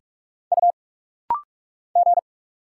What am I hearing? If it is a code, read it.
Morse code, heard as a keyed tone, about 46 wpm, 700 Hz. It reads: U G
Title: Single Letters - Rapid-Fire 45wpm